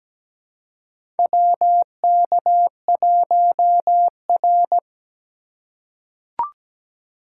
Morse code, heard as a keyed tone, about 17 wpm, 700 Hz. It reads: WK1R
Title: USA Call Signs - 2x Character Spacing - 17wpm